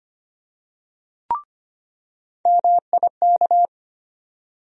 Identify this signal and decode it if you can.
Morse code, heard as a keyed tone, about 25 wpm, 700 Hz. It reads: MIK
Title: USA Call Sign Postfixes - 3 Letters 25wpm